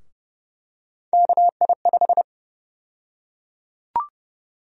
Morse code, heard as a keyed tone, about 31 wpm, 700 Hz. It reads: KI5
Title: USA Call Sign Prefixes - 30wpm